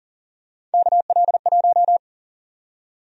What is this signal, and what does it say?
Morse code, heard as a keyed tone, about 40 wpm, 700 Hz. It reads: KL1
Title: USA Call Sign Prefixes - 40wpm